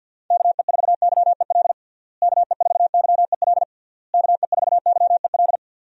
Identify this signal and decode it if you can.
Morse code, heard as a keyed tone, about 52 wpm, 700 Hz. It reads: KE4YEL KE4YEL KE4YEL
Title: USA Call Signs - 6 Characters - Repeated 3x - 50wpm